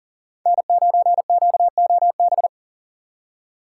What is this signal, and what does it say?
Morse code, heard as a keyed tone, about 40 wpm, 700 Hz. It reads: N9QOB
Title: USA Call Signs - 5 Characters - 3x Character Spacing - 40wpm